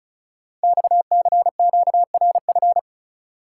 Morse code, heard as a keyed tone, about 36 wpm, 690 Hz. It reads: XCQRF
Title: Code Groups - 5 Letters - 35wpm